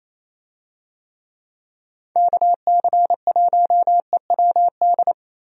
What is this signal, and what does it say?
Morse code, heard as a keyed tone, about 28 wpm, 700 Hz. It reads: KC1EWD
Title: USA Call Signs - 6 Characters - Repeated 2x - 28wpm